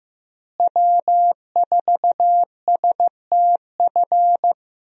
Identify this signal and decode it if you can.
Morse code, heard as a keyed tone, about 15 wpm, 700 Hz. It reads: W4STF
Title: USA Call Signs - 5 Characters - Repeated 2x - 15wpm